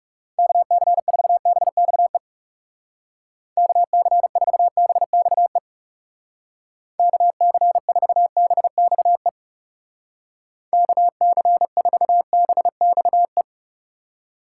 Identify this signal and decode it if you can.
Morse code, heard as a keyed tone, about 38 wpm, 700 Hz. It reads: KC4BXE KC4BXE KC4BXE KC4BXE
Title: Speed Racing - USA 2x3 Call Signs 30wpm